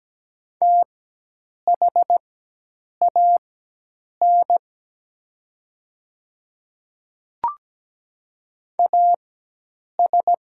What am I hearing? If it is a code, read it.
Morse code, heard as a keyed tone, about 17 wpm, spaced as for 8 wpm, 700 Hz. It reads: THAN AS
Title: 4x Character Spacing - Top 500 Words - 17wpm